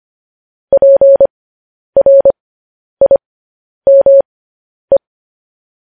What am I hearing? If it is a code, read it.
Morse code, heard as a keyed tone, about 25 wpm, spaced as for 10 wpm, 550 Hz. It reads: PRIME
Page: news headlines in morse code at 10 wpm